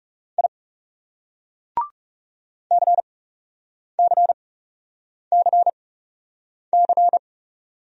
Morse code, heard as a keyed tone, about 36 wpm, spaced as for 8 wpm, 700 Hz. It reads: I CCCC
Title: Speed-Racing Single Letters 30wpm